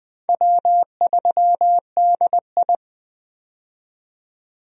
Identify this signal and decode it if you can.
Morse code, heard as a keyed tone, about 20 wpm, 700 Hz. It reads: W3DI
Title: USA Call Signs - 4 Characters - 20wpm